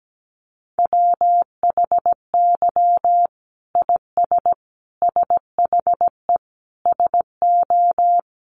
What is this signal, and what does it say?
Morse code, heard as a keyed tone, about 17 wpm, 700 Hz. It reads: WHY IS SHE SO
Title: Sentences from Top 1000 Words 17wpm